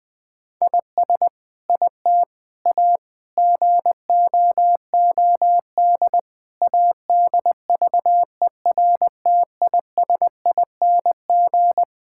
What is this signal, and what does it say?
Morse code, heard as a keyed tone, about 20 wpm, 700 Hz. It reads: IS IT A GOOD ADVERTISING
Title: Sentences from Top 1000 Words - Review Part 6 - 20wpm